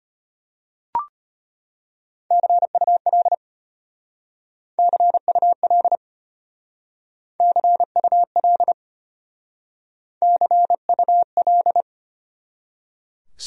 Morse code, heard as a keyed tone, about 30 wpm, 700 Hz. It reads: CUL CUL CUL CUL
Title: Speed-Racing Abbreviations - 25wpm